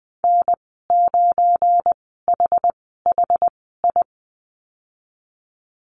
Morse code, heard as a keyed tone, about 20 wpm, spaced as for 15 wpm, 700 Hz. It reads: N9HHI
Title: 179 Lesson 29g - Slash - Call Signs 20wpm